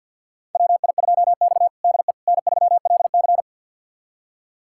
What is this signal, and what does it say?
Morse code, heard as a keyed tone, about 47 wpm, 700 Hz. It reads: WI2X DE N3LC
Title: USA Callsign DE Callsign - 4 Characters - 50wpm